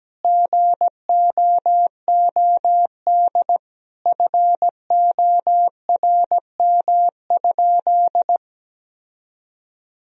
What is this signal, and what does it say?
Morse code, heard as a keyed tone, about 17 wpm, 700 Hz. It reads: GOOD FORM?
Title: Sentences from Top 1000 Words - Review Part 1 - 17wpm